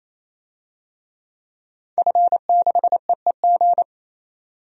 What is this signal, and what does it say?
Morse code, heard as a keyed tone, about 28 wpm, 700 Hz. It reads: F6EEG
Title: DXCC-100 Call Signs - 5 Characters - 28wpm